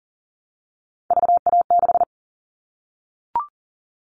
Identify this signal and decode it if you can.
Morse code, heard as a keyed tone, about 40 wpm, 700 Hz. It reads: VA6
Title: Callsign Prefixes for Top 25 DXCC - 40wpm